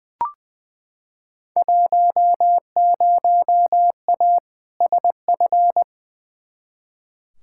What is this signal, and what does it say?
Morse code, heard as a keyed tone, about 21 wpm, 700 Hz. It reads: 10A SF
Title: Field Day Exchanges Encore - 20wpm